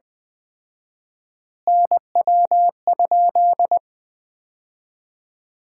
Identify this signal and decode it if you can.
Morse code, heard as a keyed tone, about 20 wpm, 700 Hz. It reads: NW?
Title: Common QSO Phrases - 4x Word-Spacing - 20wpm